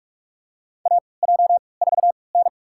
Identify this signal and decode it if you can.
Morse code, heard as a keyed tone, about 46 wpm, spaced as for 26 wpm, 700 Hz. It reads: AJ4N
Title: USA Call Signs - 3x Character Spacing - 45wpm